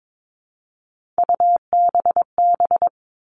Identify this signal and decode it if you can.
Morse code, heard as a keyed tone, about 22 wpm, 700 Hz. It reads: UBB
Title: All Cut Numbers - 3-Digit 22wpm